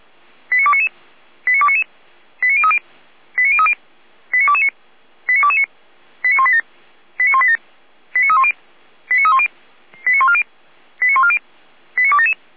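Five-tone selective call (ZVEI), heard as quick runs of five tones, tones of about 70 ms. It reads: 89299 89299 89030 89030 89229 89229 89187 89187 89310 89310 89140 89140 89277
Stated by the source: ZVEI five-tone call received by SDR.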